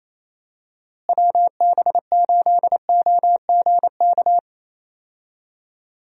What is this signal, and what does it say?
Morse code, heard as a keyed tone, about 28 wpm, 700 Hz. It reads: WB8OGK
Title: USA Call Signs - 6 Characters - Repeated 3x - 28wpm